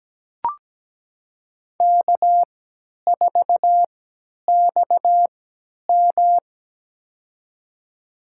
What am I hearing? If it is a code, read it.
Morse code, heard as a keyed tone, about 17 wpm, spaced as for 10 wpm, 700 Hz. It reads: K4XM
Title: USA Call Signs - 4 Characters - 3x Character Spacing - 17wpm